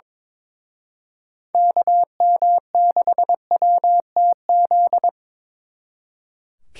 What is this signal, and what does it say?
Morse code, heard as a keyed tone, about 24 wpm, 700 Hz. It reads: KM6WTZ
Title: Speed Racing - USA Call Signs 22wpm